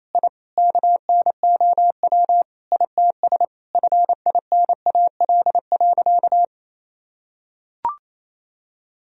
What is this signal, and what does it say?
Morse code, heard as a keyed tone, about 28 wpm, 700 Hz. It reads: I KNOW ITS FINAL.
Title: Sentences from Top 1000 Words - Review Part 2 - 28wpm